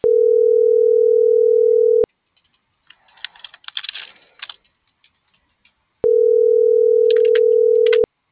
Ringback tone (North America), 440 + 480 Hz.